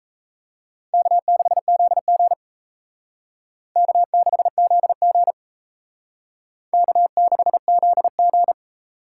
Morse code, heard as a keyed tone, about 37 wpm, 700 Hz. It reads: K6ZG K6ZG K6ZG